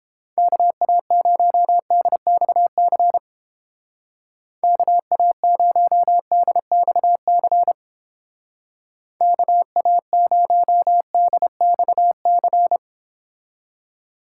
Morse code, heard as a keyed tone, about 29 wpm, 700 Hz. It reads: KA0DXC KA0DXC KA0DXC